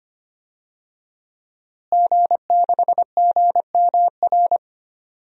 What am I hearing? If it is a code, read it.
Morse code, heard as a keyed tone, about 25 wpm, 700 Hz. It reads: G6GMR
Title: DXCC-100 Call Signs - 5 Characters - Repeated 2x - 25wpm